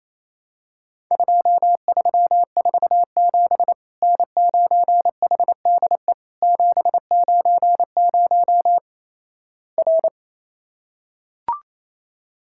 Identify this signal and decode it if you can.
Morse code, heard as a keyed tone, about 27 wpm, 700 Hz. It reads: 2347 N9HDE 790 R
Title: Worked All Europe DX Contest - QTC Exchange - 28wpm